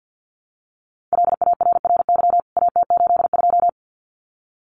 Morse code, heard as a keyed tone, about 50 wpm, 700 Hz. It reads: LARRY AT82
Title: CWT Exchange 50wpm